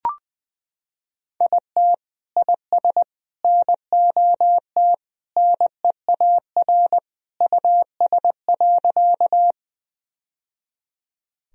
Morse code, heard as a keyed tone, about 20 wpm, 700 Hz. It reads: IT IS NOT NEAR US.
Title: Sentences from Top 1000 Words - Review Part 1 - 20wpm